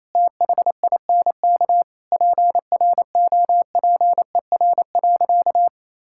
Morse code, heard as a keyed tone, about 28 wpm, 700 Hz. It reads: THINK PROPER.
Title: Sentences from Top 1000 Words - Review Part 5 - 28wpm